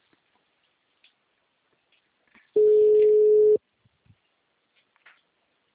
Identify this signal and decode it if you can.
Ringback tone (Europe, 425 Hz).
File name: out- -2051- - - .43388